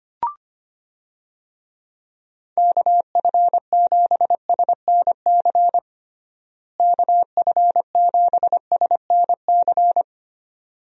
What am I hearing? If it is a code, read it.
Morse code, heard as a keyed tone, about 25 wpm, 700 Hz. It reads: KF7SNC KF7SNC
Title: USA Call Signs - 6 Characters - Repeated 2x - 25wpm